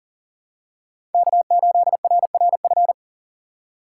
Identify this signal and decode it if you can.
Morse code, heard as a keyed tone, about 40 wpm, 700 Hz. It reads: K8RRF
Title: USA Call Signs - 5 Characters - 2x Character Spacing - 40wpm